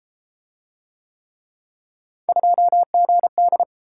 Morse code, heard as a keyed tone, about 32 wpm, 700 Hz. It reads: 2GD